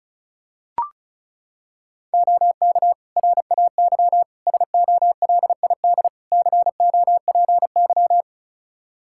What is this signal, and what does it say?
Morse code, heard as a keyed tone, about 34 wpm, 700 Hz. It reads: OK RAY SOLID COPY
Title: Practice QSO - Exchange 3 - More Info - 35wpm